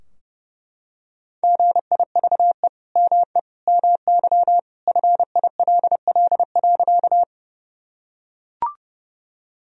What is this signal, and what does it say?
Morse code, heard as a keyed tone, about 32 wpm, 700 Hz. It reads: GIVE ME MY FILL.
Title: Sentences from Top 1000 Words - Review Part 3 - 30wpm